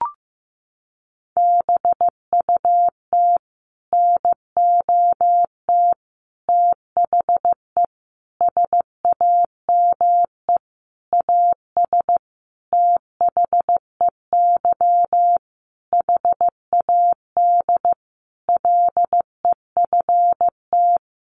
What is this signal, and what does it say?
Morse code, heard as a keyed tone, about 15 wpm, 700 Hz. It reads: BUT NOT THE SAME AS THEY HAD LEFT
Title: A Sound Of Thunder 15wpm